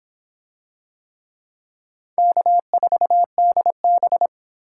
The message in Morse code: K4DB

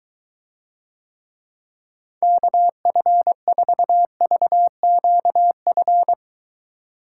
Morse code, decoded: KF4VQF

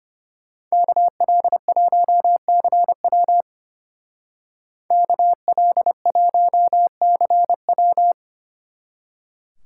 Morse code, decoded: KL1CW KL1CW

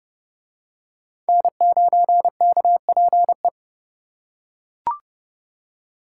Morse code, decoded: N9KPE

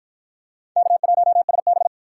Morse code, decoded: K1SB

